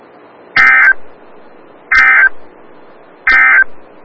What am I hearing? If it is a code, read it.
Emergency Alert System (SAME) tones, heard as harsh buzzing bursts. It is heard from 0.4 to 3.7 s.
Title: Des Moines, IA NOAA Weather Radio - WXL57